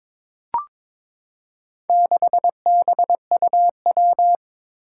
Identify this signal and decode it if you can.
Morse code, heard as a keyed tone, about 22 wpm, 700 Hz. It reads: 6BUW
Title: Four Challenging Letter-Number Combinations 22wpm